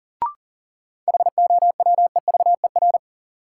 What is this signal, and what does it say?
Morse code, heard as a keyed tone, about 40 wpm, 700 Hz. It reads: HOWEVER
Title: Sentences from Top 600 Words - New Word Drill - 40wpm